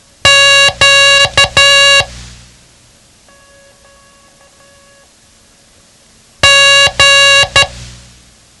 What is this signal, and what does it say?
Morse code, heard as a keyed tone, about 12 wpm, 2800 Hz. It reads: Q G
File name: ham radio / CodeQuick / Code Quick 07